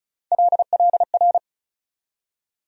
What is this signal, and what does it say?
Morse code, heard as a keyed tone, about 34 wpm, 700 Hz. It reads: LLR
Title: USA Call Sign Postfixes - 3 Letters 35wpm